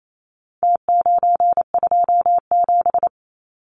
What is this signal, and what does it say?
Morse code, heard as a keyed tone, about 28 wpm, 700 Hz. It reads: T927
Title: Numbers - Contest Style - 4-Digit 28wpm